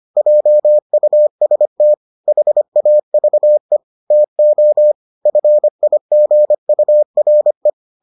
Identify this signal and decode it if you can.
Morse code, heard as a keyed tone, about 25 wpm, 600 Hz. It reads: JUST HAVE TO FIGURE